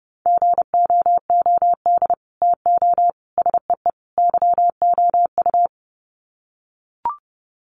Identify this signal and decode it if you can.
Morse code, heard as a keyed tone, about 30 wpm, 700 Hz. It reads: GOOD TO SEE YOU
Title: Sets of 4 Words 30wpm